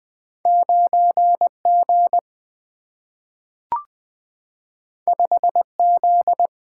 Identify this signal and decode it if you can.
Morse code, heard as a keyed tone, about 20 wpm, 700 Hz. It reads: 9 G 5 Z
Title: Single-Letter Single-Number 20wpm